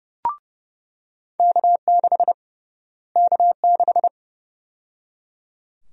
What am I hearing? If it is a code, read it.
Morse code, heard as a keyed tone, about 32 wpm, 700 Hz. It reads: K6 K6
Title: USA Call Sign Prefixes - Repeated 2x - 30wpm